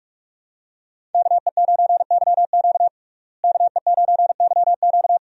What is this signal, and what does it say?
Morse code, heard as a keyed tone, about 46 wpm, 700 Hz. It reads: KE9YQ KE9YQ
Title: USA Call Signs - 5 Characters - Repeated 2x - 45wpm